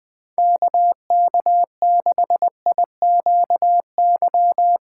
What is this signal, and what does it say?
Morse code, heard as a keyed tone, about 20 wpm, 700 Hz. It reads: KK6IQY
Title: USA Call Signs - 6 Characters - 3x Character Spacing - 20wpm